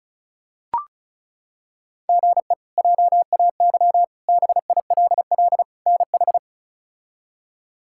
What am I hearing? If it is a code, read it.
Morse code, heard as a keyed tone, about 34 wpm, 700 Hz. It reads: GE JAY BILL NH
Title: K1USN SST QSO Practice - 35wpm